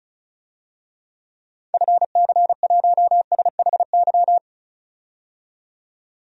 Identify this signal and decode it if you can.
Morse code, heard as a keyed tone, about 36 wpm, 700 Hz. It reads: FC1SHY
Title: DXCC-100 Call Signs - 6 Characters - 35wpm